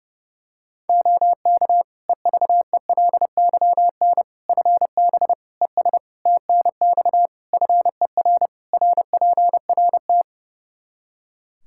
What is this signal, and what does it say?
Morse code, heard as a keyed tone, about 32 wpm, 700 Hz. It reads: OK EVELYN FB ES TNX FER RPRT
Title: Practice QSO - Exchange 2 - Info - 30wpm